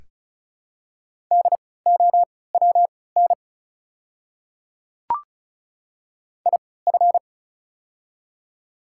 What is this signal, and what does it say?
Morse code, heard as a keyed tone, about 34 wpm, spaced as for 20 wpm, 700 Hz. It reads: DOWN IF
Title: 3x Character Spacing - Top 100 Words - 35wpm